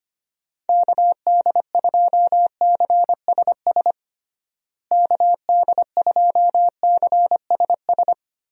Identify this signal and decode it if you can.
Morse code, heard as a keyed tone, about 25 wpm, 700 Hz. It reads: KD2CSS KD2CSS
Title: USA Call Signs - 6 Characters - Repeated 2x - 25wpm